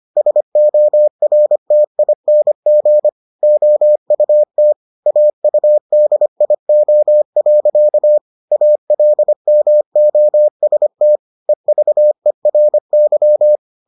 Morse code, heard as a keyed tone, about 25 wpm, 600 Hz. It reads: SORTING OUT AUDIO. ALMOST EVERY